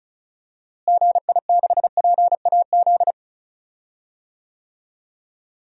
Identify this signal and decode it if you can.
Morse code, heard as a keyed tone, about 36 wpm, 700 Hz. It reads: GI6PAZ
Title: DXCC-100 Call Signs - 6 Characters - Repeated 3x - 35wpm